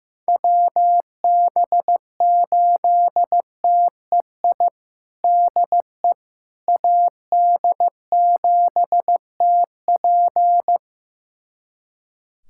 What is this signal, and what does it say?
Morse code, heard as a keyed tone, about 15 wpm, 700 Hz. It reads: WB8TEI DE AD7TP
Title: USA Callsign DE Callsign 15wpm